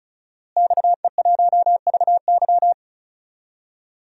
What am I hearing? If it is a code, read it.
Morse code, heard as a keyed tone, about 36 wpm, 700 Hz. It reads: XE1VY
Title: DXCC-100 Call Signs - 5 Characters - Repeated 2x - 35wpm